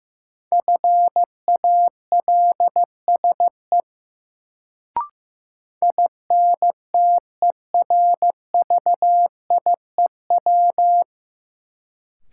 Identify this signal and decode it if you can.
Morse code, heard as a keyed tone, about 16 wpm, 700 Hz. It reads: FALSE INTERVIEW
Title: Sentences from Top 1600 Words - New Word Drill - 15wpm